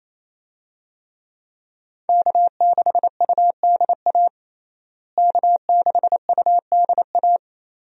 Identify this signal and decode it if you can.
Morse code, heard as a keyed tone, about 28 wpm, 700 Hz. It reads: K6UDA K6UDA
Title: USA Call Signs - 5 Characters - Repeated 2x - 28wpm